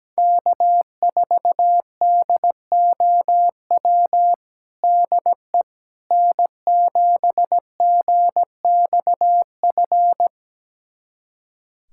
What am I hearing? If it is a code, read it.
Morse code, heard as a keyed tone, about 17 wpm, 700 Hz. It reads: K4DOW DE N7GXF
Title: USA Callsign DE Callsign - 5 Characters - 17wpm